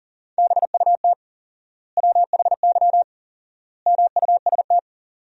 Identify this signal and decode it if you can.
Morse code, heard as a keyed tone, about 40 wpm, 700 Hz. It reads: BUT WHY MUST
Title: 4x Word-Spacing - Sentences from Top 400 Words - 40wpm